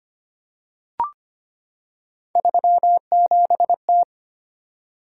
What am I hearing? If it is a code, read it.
Morse code, heard as a keyed tone, about 25 wpm, 700 Hz. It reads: 37T
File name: Three Letter-Number Combinations - 25wpm